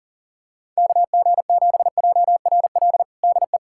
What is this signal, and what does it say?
Morse code, heard as a keyed tone, about 37 wpm, 700 Hz. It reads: KG7JRL DE
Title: USA Callsign DE Callsign - 6 Characters - 40wpm